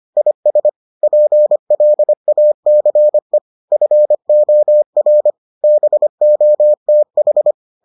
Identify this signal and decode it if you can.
Morse code, heard as a keyed tone, about 25 wpm, 600 Hz. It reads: IS PLACE FOR BOTH